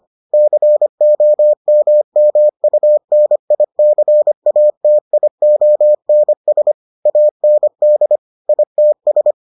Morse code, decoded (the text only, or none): COMMUNICATIONS AND ITS